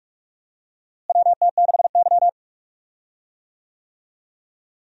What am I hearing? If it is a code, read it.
Morse code, heard as a keyed tone, about 45 wpm, 700 Hz. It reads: WT6Y
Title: USA Call Signs - 4 Characters - Repeated 2x - 45wpm